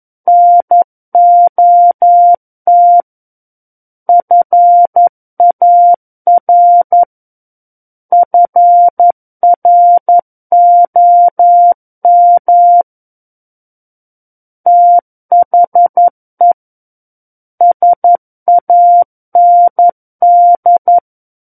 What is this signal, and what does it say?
Morse code, heard as a keyed tone, about 11 wpm, 700 Hz. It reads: NOT FAR FROM THE SAND
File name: Book One / 02-Chapter 2 - 11 WPM